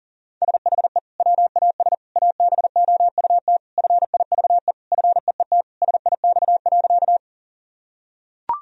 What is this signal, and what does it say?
Morse code, heard as a keyed tone, about 40 wpm, 700 Hz. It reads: SHE WAS ABOUT FIVE FEET SIX.
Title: Sentences from Top 1000 Words - Review Part 2 - 40wpm